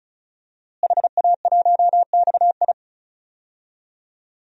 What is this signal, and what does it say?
Morse code, heard as a keyed tone, about 36 wpm, 700 Hz. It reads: HA1XI
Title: DXCC-100 Call Signs - 5 Characters - Repeated 3x - 35wpm